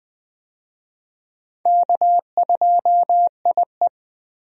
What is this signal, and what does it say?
Morse code, heard as a keyed tone, about 20 wpm, 700 Hz. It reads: K2IE
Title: USA Call Signs - 4 Characters - Repeated 3x - 20wpm